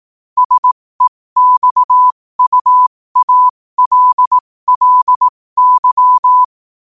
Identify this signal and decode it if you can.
Morse code, heard as a keyed tone, about 18 wpm, spaced as for 15 wpm, 1000 Hz. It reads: SEXUALLY
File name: 15wpm